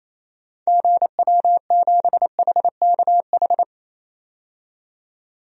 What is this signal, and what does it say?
Morse code, heard as a keyed tone, about 28 wpm, 700 Hz. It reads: GW7HKH
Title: DXCC-100 Call Signs - 6 Characters - 28wpm